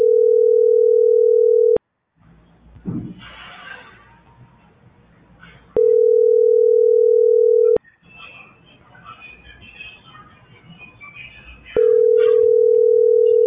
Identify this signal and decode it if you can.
Ringback tone (North America), 440 + 480 Hz.